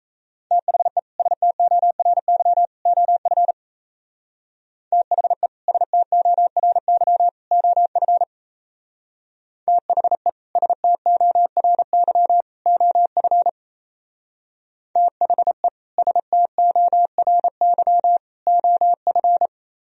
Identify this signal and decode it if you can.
Morse code, heard as a keyed tone, about 37 wpm, 700 Hz. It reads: THE STORY OF THE STORY OF THE STORY OF THE STORY OF